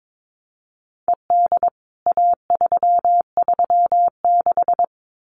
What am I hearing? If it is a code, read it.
Morse code, heard as a keyed tone, about 22 wpm, 700 Hz. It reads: ED A336
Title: CWT Exchange 22wpm